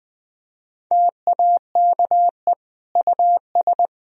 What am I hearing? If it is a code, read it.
Morse code, heard as a keyed tone, about 20 wpm, 700 Hz. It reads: TAKE US
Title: Sentences from Top 1000 Words - Review Part 1 - 20wpm